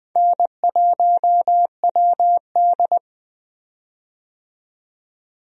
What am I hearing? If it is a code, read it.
Morse code, heard as a keyed tone, about 21 wpm, 700 Hz. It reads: N1WD
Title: USA Call Signs - 4 Characters - 20wpm